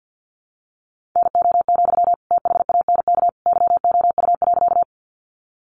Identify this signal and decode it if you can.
Morse code, heard as a keyed tone, about 49 wpm, 700 Hz. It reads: NO, THANK YOU.
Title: Sentences from Top 600 Words 50wpm